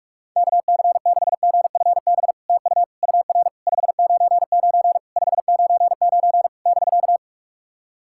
Morse code, heard as a keyed tone, about 46 wpm, 700 Hz. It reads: KC6GFB TU UR 599 599 <BK>
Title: SOTA QSO Practice 45wpm